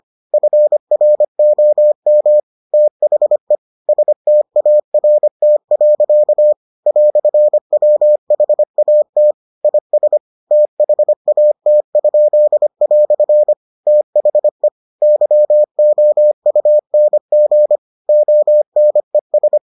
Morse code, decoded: FROM THE START. "WHAT IS THAT?" THE YOUNG ONES

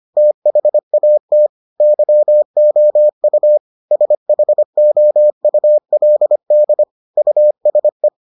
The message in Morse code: THAT YOU SHOULD USE

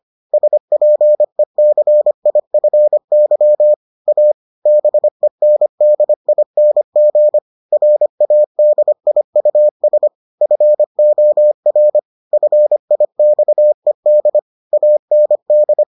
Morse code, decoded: SPECIFY A BENDING RADIUS FOR FIXED AND

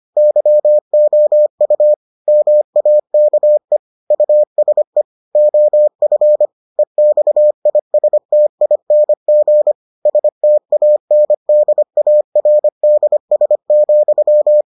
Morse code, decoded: YOU MAKE USE OF EXISTING STANDARDS,

